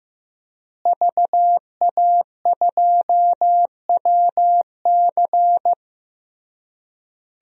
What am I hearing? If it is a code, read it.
Morse code, heard as a keyed tone, about 15 wpm, 700 Hz. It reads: VA2WC
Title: DXCC-100 Call Signs - 5 Characters - Repeated 3x - 15wpm